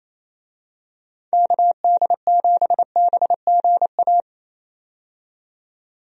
Morse code, heard as a keyed tone, about 27 wpm, 700 Hz. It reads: KD7BGA
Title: USA Call Signs - 6 Characters - 28wpm